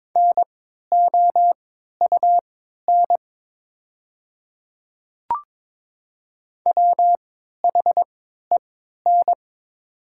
Morse code, heard as a keyed tone, about 22 wpm, spaced as for 12 wpm, 700 Hz. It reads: NOUN WHEN
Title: 3x Character Spacing - Top 500 Words - 22wpm